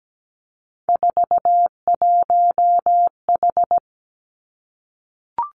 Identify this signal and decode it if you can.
Morse code, heard as a keyed tone, about 17 wpm, 700 Hz. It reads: 41H
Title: Three Challenging Letter-Number Combinations 17wpm